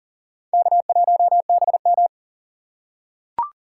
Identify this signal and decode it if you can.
Morse code, heard as a keyed tone, about 40 wpm, 700 Hz. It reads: K1BM
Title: USA Call Signs - 4 Characters - 3x Character Spacing - 40wpm